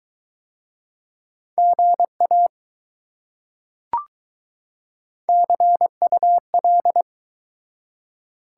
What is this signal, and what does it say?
Morse code, heard as a keyed tone, about 21 wpm, 700 Hz. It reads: GA CUL